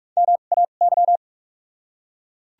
Morse code, heard as a keyed tone, about 41 wpm, spaced as for 31 wpm, 700 Hz. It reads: MAY
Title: 2x Character Spacing - Top 200 Words - 45wpm